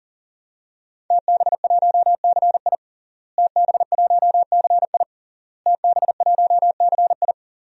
Morse code, heard as a keyed tone, about 37 wpm, 700 Hz. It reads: TB1CI TB1CI TB1CI
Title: DXCC-100 Call Signs - 5 Characters - Repeated 3x - 40wpm